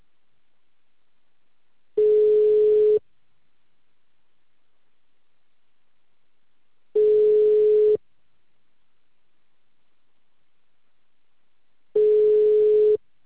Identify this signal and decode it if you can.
Ringback tone (Europe, 425 Hz).